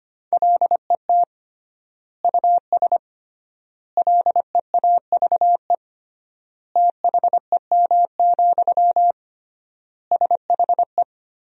Morse code, heard as a keyed tone, about 25 wpm, 700 Hz. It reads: LET US LEAVE THEM, SHE
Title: 3x Word-Spacing - Sentences from Top 400 Words - 25wpm